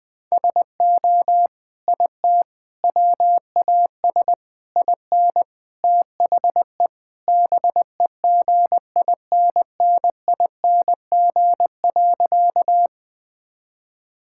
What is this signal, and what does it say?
Morse code, heard as a keyed tone, about 20 wpm, 700 Hz. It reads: SO IT WAS IN THE BEGINNING.
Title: Sentences from Top 1000 Words - Review Part 5 - 20wpm